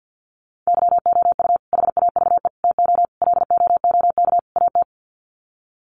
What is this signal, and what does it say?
Morse code, heard as a keyed tone, about 50 wpm, 700 Hz. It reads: YOU HAVE TO LOOK AT
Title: Sets of 5 Words Encore 50wpm